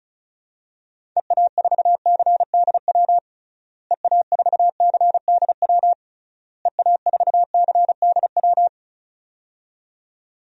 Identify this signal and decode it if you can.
Morse code, heard as a keyed tone, about 36 wpm, 700 Hz. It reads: EA4CDW EA4CDW EA4CDW
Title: DXCC-100 Call Signs - 6 Characters - Repeated 3x - 35wpm